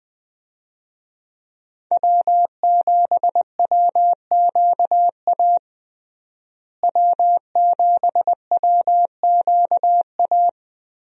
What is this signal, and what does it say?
Morse code, heard as a keyed tone, about 20 wpm, 700 Hz. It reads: W7WQA W7WQA
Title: USA Call Signs - 5 Characters - Repeated 3x - 20wpm